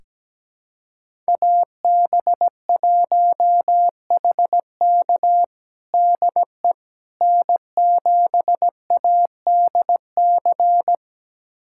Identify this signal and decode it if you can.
Morse code, heard as a keyed tone, about 17 wpm, 700 Hz. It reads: AB1HK DE N7ADC
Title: USA Callsign DE Callsign - 5 Characters - 17wpm